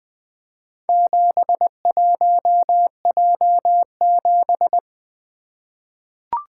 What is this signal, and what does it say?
Morse code, heard as a keyed tone, about 20 wpm, 700 Hz. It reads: 71J7
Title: Four Letter-Number Combinations - 20wpm